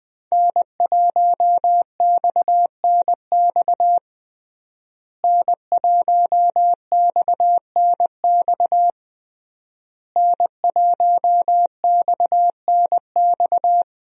Morse code, heard as a keyed tone, about 21 wpm, 700 Hz. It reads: N1XNX N1XNX N1XNX
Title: USA Call Signs - 5 Characters - Repeated 3x - 20wpm